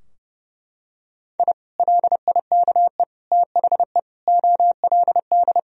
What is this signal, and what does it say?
Morse code, heard as a keyed tone, about 30 wpm, 700 Hz. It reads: I LIKE THE OLD
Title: Sentences from Top 1000 Words - Review Part 2 - 30wpm